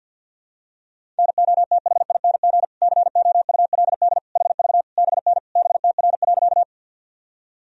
Morse code, heard as a keyed tone, about 46 wpm, 700 Hz. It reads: NOTHING COULD HV BN BTR.
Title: Sentences from Top 500 Words and Abbreviations 50wpm